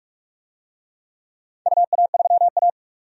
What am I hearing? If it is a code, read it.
Morse code, heard as a keyed tone, about 44 wpm, 700 Hz. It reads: UA3A
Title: DXCC-100 Call Signs - 4 Characters - 45wpm